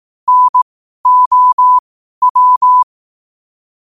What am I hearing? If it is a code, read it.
Morse code, heard as a keyed tone, about 18 wpm, spaced as for 12 wpm, 1000 Hz. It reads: NOW
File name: qotdfiles / 12wpm